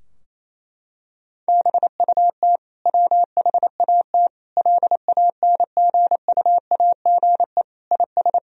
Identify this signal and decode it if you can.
Morse code, heard as a keyed tone, about 28 wpm, 700 Hz. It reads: BUT WHAT LANGUAGE IS
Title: Sentences from Top 1000 Words - Review Part 3 - 28wpm